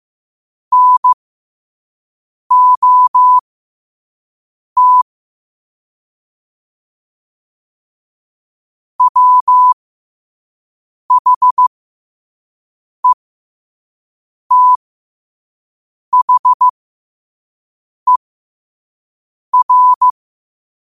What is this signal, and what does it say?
Morse code, heard as a keyed tone, about 15 wpm, spaced as for 5 wpm, 1000 Hz. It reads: NOT WHETHER
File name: qotdfiles / 5wpm